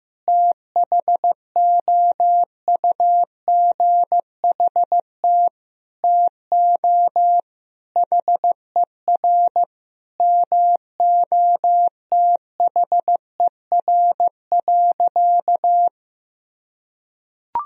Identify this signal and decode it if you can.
Morse code, heard as a keyed tone, about 15 wpm, 700 Hz. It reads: THOUGHT TO HER MOTHER.
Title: Sentences from Top 1000 Words - Review Part 2 - 15wpm